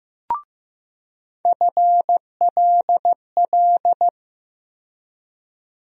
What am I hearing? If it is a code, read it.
Morse code, heard as a keyed tone, about 15 wpm, 700 Hz. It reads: FLL
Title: USA Call Sign Postfixes - 3 Letters 15wpm